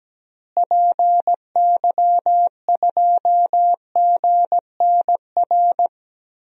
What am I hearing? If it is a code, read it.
Morse code, heard as a keyed tone, about 17 wpm, 700 Hz. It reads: PY2GNR
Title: DXCC-100 Call Signs - 6 Characters - Repeated 2x - 17wpm